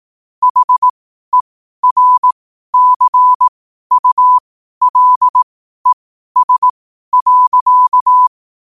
Morse code, heard as a keyed tone, about 18 wpm, spaced as for 12 wpm, 1000 Hz. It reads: HERCULES.
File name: qotdfiles / 12wpm